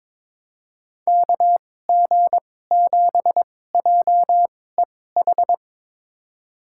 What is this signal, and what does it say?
Morse code, heard as a keyed tone, about 21 wpm, spaced as for 16 wpm, 700 Hz. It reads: KG7JEH
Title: USA Call Signs - 6 Characters - 2x Character Spacing - 22wpm